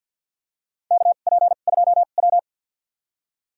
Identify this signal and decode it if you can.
Morse code, heard as a keyed tone, about 48 wpm, spaced as for 35 wpm, 700 Hz. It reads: KP2W
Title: USA Call Signs - 2x Character Spacing - 50wpm